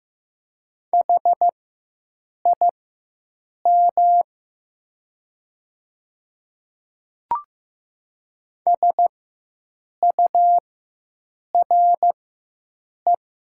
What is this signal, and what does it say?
Morse code, heard as a keyed tone, about 15 wpm, spaced as for 7 wpm, 700 Hz. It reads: HIM SURE
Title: 4x Character Spacing - Top 300 Words - 15wpm